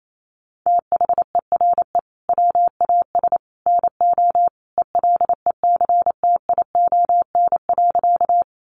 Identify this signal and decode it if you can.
Morse code, heard as a keyed tone, about 28 wpm, 700 Hz. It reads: THERE WAS NO ELECTION.
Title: Sentences from Top 1500 Words 28wpm